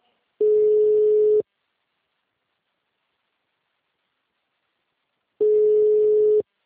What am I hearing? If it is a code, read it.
Ringback tone (Europe, 425 Hz).
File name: out- -2013- - - .34272